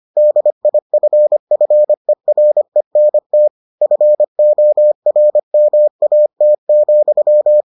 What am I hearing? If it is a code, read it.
Morse code, heard as a keyed tone, about 25 wpm, 600 Hz. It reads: DIFFERENT FORMAT,